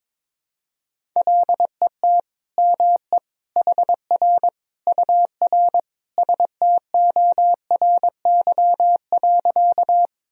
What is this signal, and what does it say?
Morse code, heard as a keyed tone, about 22 wpm, 700 Hz. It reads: LET ME HR UR STORY.
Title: Sentences from Top 300 Words and Abbreviations 22wpm